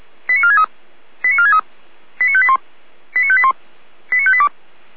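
Five-tone selective call (ZVEI), heard as quick runs of five tones, tones of about 80 ms. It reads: 89463 89463 89581 89581 89582
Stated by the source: ZVEI five-tone selective call received by SDR.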